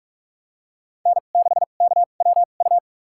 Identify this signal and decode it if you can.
Morse code, heard as a keyed tone, about 46 wpm, spaced as for 32 wpm, 700 Hz. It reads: N6KWU
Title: USA Call Signs - 2x Character Spacing - 45wpm